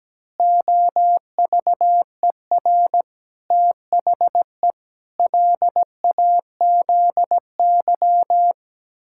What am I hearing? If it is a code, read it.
Morse code, heard as a keyed tone, about 17 wpm, 700 Hz. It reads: OVER THE LAZY
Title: Pangram Challenge v2 - 17wpm